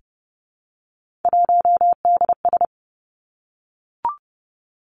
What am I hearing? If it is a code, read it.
Morse code, heard as a keyed tone, about 30 wpm, 700 Hz. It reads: 1DS
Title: Three Challenging Letter-Number Combinations 30wpm